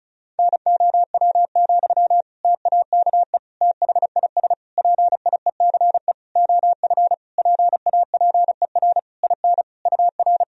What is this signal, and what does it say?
Morse code, heard as a keyed tone, about 34 wpm, 700 Hz. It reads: NOW, TAKE THIS PIECE OF PAPER IN UR
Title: Sentences from Top 400 Words and Abbreviations 35wpm